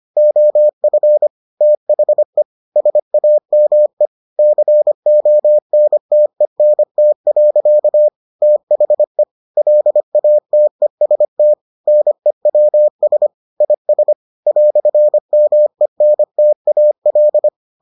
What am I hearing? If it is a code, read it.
Morse code, heard as a keyed tone, about 25 wpm, 600 Hz. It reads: OF THE SAME CONTENT. THE LATEST NEWS IS "MENTAL